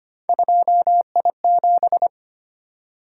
Morse code, heard as a keyed tone, about 29 wpm, 700 Hz. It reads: 2I7
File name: Speed-Racing - Three Letter-Number Combinations - 25wpm